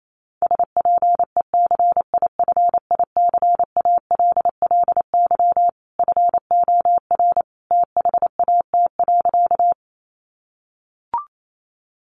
Morse code, heard as a keyed tone, about 28 wpm, 700 Hz. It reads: SPECIFICALLY FOR THAT.
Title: Sentences from Top 1100 Words 28wpm